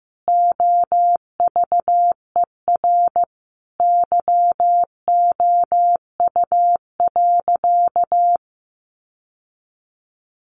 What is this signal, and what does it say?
Morse code, heard as a keyed tone, about 15 wpm, 700 Hz. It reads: OVER YOU.
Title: Sentences from Top 900 Words 15wpm